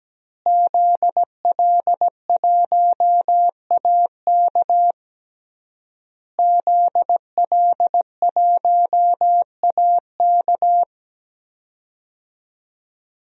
Morse code, heard as a keyed tone, about 17 wpm, 700 Hz. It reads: ZL1AK ZL1AK
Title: DXCC-100 Call Signs - 5 Characters - Repeated 3x - 17wpm